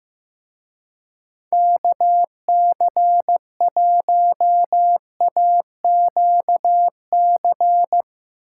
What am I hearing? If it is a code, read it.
Morse code, heard as a keyed tone, about 15 wpm, 700 Hz. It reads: KC1AQC
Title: Speed Racing - USA Call Signs 15wpm